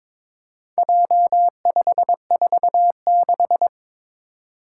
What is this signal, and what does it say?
Morse code, heard as a keyed tone, about 22 wpm, 700 Hz. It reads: J546
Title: Four Challenging Letter-Number Combinations 22wpm